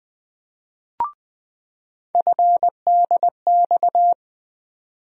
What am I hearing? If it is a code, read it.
Morse code, heard as a keyed tone, about 20 wpm, 700 Hz. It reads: FDX
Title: USA Call Sign Postfixes - 3 Letters 20wpm